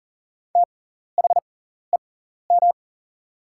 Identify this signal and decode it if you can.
Morse code, heard as a keyed tone, about 40 wpm, spaced as for 14 wpm, 700 Hz. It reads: THEM
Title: 6x Character Spacing - Top 100 Words - 40wpm